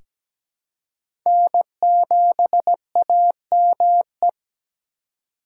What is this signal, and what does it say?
Morse code, heard as a keyed tone, about 17 wpm, 700 Hz. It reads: N7AME
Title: K1USN SST QSO Practice - 17wpm